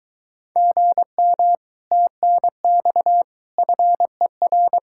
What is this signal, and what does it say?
Morse code, heard as a keyed tone, about 23 wpm, 700 Hz. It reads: GM TNX FER